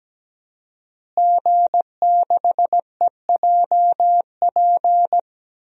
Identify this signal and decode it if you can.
Morse code, heard as a keyed tone, about 17 wpm, 700 Hz. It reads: G6EJP
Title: DXCC-100 Call Signs - 5 Characters - Repeated 2x - 17wpm